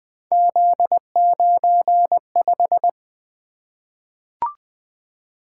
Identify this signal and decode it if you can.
Morse code, heard as a keyed tone, about 20 wpm, 700 Hz. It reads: Z95
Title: Three Letter-Number Combinations - 20wpm